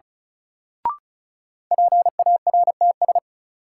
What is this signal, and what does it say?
Morse code, heard as a keyed tone, about 36 wpm, 700 Hz. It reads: PARTS
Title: Sets of 3 Words - New Word Drill - 35wpm